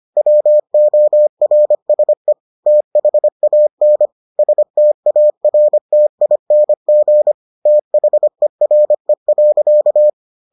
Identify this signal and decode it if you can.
Morse code, heard as a keyed tone, about 25 wpm, 600 Hz. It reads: WORSE THAN STARTING THERE.